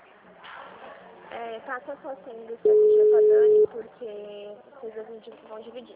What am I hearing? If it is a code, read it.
Ringback tone (Europe, 425 Hz), with speech.